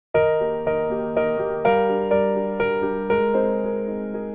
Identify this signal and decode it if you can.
There is no code.